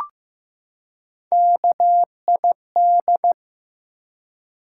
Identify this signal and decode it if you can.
Morse code, heard as a keyed tone, about 15 wpm, 700 Hz. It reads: KID
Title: Three Letter Words - 15wpm